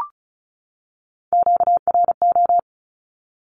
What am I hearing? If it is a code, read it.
Morse code, heard as a keyed tone, about 34 wpm, 700 Hz. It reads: QRO
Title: Q-Codes 35wpm